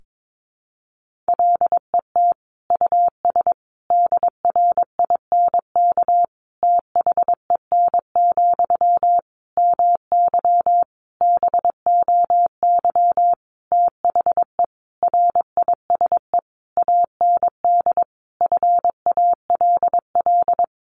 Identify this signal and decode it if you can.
Morse code, heard as a keyed tone, about 22 wpm, 700 Hz. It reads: LET US DRINK THEN, MY BOY THE RISE AND FALL
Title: Sentences from Top 1500 Words 22wpm